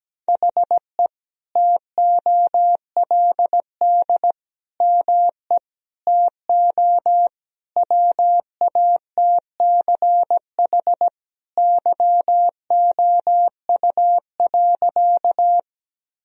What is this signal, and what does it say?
Morse code, heard as a keyed tone, about 17 wpm, 700 Hz. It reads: HE TOLD ME TO WATCH YOU.
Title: Sentences from Top 1000 Words - Review Part 1 - 17wpm